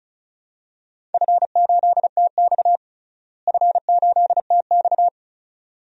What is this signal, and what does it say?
Morse code, heard as a keyed tone, about 36 wpm, 700 Hz. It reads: F8TX F8TX
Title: DXCC-100 Call Signs - 4 Characters - Repeated 2x - 35wpm